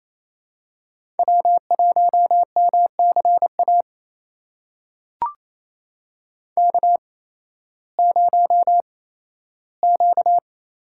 Morse code, heard as a keyed tone, about 27 wpm, spaced as for 15 wpm, 700 Hz. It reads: W1MCA K0Q
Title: USA Call Signs - 5 Characters - 8x Character Spacing - 28wpm